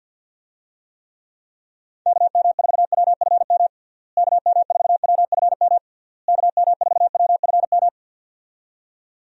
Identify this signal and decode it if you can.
Morse code, heard as a keyed tone, about 50 wpm, 700 Hz. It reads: KM4WFM KM4WFM KM4WFM
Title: USA Call Signs - 6 Characters - Repeated 3x - 50wpm